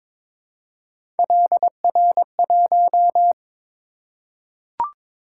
Morse code, heard as a keyed tone, about 22 wpm, 700 Hz. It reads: LR1
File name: practice / 22wpm / Three Letter-Number Combinations - 22wpm